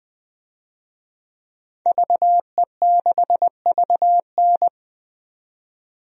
Morse code, heard as a keyed tone, about 20 wpm, 700 Hz. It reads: VE6VN
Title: DXCC-100 Call Signs - 5 Characters - 20wpm